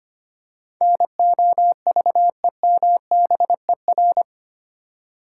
Morse code, heard as a keyed tone, about 25 wpm, 690 Hz. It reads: NOVEMBER
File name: 8 Letter Words - First 3 Letters Given - 25wpm